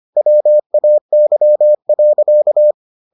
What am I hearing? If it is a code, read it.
Morse code, heard as a keyed tone, about 25 wpm, 600 Hz. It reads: WAY.